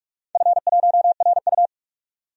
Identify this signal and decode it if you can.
Morse code, heard as a keyed tone, about 46 wpm, 700 Hz. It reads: F1RU
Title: DXCC-100 Call Signs - 4 Characters - 45wpm